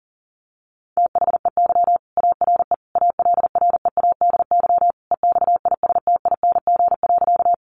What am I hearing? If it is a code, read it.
Morse code, heard as a keyed tone, about 40 wpm, 700 Hz. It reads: THEY ARE ALREADY EXISTING.
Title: Sentences from Top 1500 Words 40wpm